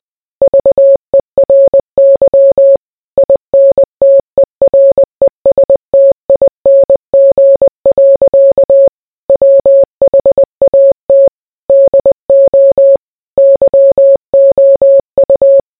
Morse code, heard as a keyed tone, about 20 wpm, 550 Hz. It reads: VERY INTERESTING. WHAT DO YOU